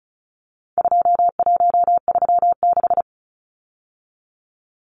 Morse code, heard as a keyed tone, about 34 wpm, 700 Hz. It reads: 2136
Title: Numbers - 4-Digit 35wpm